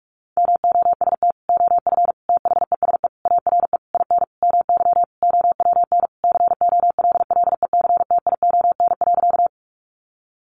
Morse code, heard as a keyed tone, about 44 wpm, 700 Hz. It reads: MOST OF THESE ARE IN MY OWN COLLECTION.
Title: Sentences from Top 1200 Words 45wpm